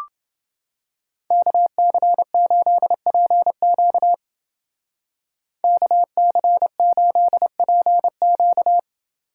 Morse code, heard as a keyed tone, about 29 wpm, 700 Hz. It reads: KC8PQ KC8PQ